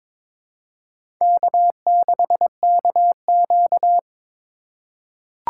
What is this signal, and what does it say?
Morse code, heard as a keyed tone, about 22 wpm, 700 Hz. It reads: K6KQ
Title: USA Call Signs - 4 Characters - 4x Character Spacing - 22wpm